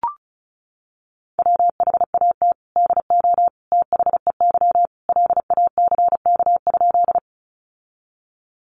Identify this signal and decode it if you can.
Morse code, heard as a keyed tone, about 34 wpm, 700 Hz. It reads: WHAT DO THEY LACK?
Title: Sentences from Top 1100 Words 35wpm